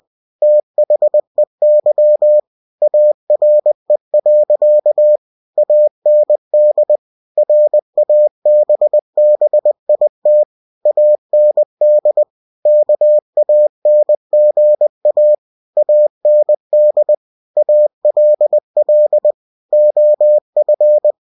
Morse code, heard as a keyed tone, about 20 wpm, 600 Hz. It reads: THEY ARE. AND RABBIT AND KANGA AND ALL OF
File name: cw-books / winnie-the-pooh / winnie-20wpm-page-0077